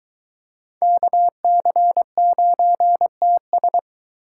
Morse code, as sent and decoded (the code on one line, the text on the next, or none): -.- -.-. ----. - ...
KC9TS